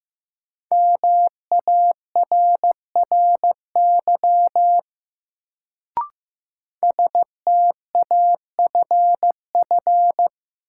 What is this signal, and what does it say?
Morse code, heard as a keyed tone, about 15 wpm, 700 Hz. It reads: MARRY STAFF
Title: Sentences from Top 1200 Words - New Word Drill Alt - 15wpm